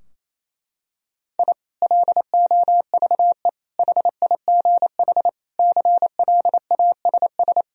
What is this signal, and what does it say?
Morse code, heard as a keyed tone, about 28 wpm, 700 Hz. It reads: I LOVE HIGH CLASS
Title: Sentences from Top 1000 Words - Review Part 4 - 28wpm